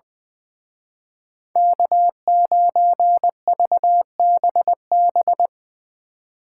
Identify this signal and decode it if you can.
Morse code, heard as a keyed tone, about 22 wpm, 700 Hz. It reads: K9VBB
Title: Speed Racing - USA 1x3 Call Signs 20wpm